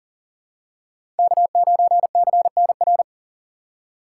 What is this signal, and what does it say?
Morse code, heard as a keyed tone, about 40 wpm, 700 Hz. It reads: K9CNR
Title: USA Call Signs - 5 Characters - 4x Character Spacing - 40wpm